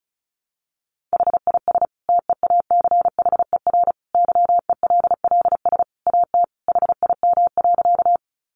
Morse code, heard as a keyed tone, about 34 wpm, 700 Hz. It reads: HIS TEACHER YELLS AT HIM.
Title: Additional Easy Phrases 35wpm